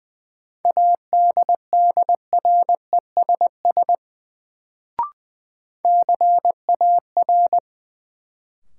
Morse code, heard as a keyed tone, about 20 wpm, 700 Hz. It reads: ADDRESS CAR
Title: Sets of 5 Words Encore - New Word Drill - 20wpm